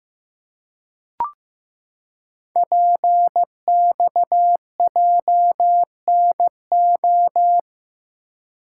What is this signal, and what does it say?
Morse code, heard as a keyed tone, about 15 wpm, 700 Hz. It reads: PXJNO
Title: Code Groups - 5 Letters - 15wpm